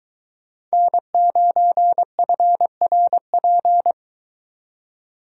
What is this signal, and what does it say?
Morse code, heard as a keyed tone, about 22 wpm, 700 Hz. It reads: N9FRP